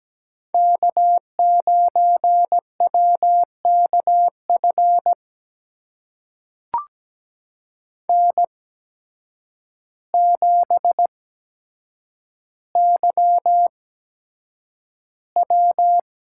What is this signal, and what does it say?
Morse code, heard as a keyed tone, about 17 wpm, spaced as for 8 wpm, 700 Hz. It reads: K9WKF N7YW
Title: USA Call Signs - 5 Characters - 8x Character Spacing - 17wpm